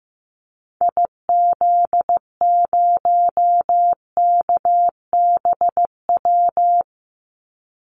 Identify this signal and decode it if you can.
Morse code, heard as a keyed tone, about 15 wpm, 700 Hz. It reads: IZ0KBW
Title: Call Signs Repeated 2x - 15wpm